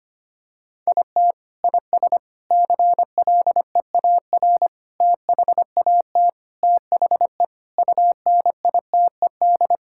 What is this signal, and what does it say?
Morse code, heard as a keyed tone, about 25 wpm, 700 Hz. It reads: IT IS CLEAR THAT THE UNITED
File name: practice / 25wpm / Sentences with 4-Word Phrases - 25wpm